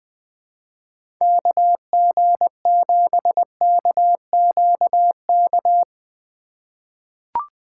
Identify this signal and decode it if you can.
Morse code, heard as a keyed tone, about 20 wpm, 700 Hz. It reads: KG7KQK
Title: USA Call Signs - 6 Characters - 2x Character Spacing - 20wpm